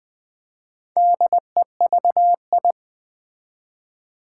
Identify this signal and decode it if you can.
Morse code, heard as a keyed tone, about 20 wpm, 700 Hz. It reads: DEVI
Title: Build A Word - 1 Letter At A Time - 5-Letter Words - 20wpm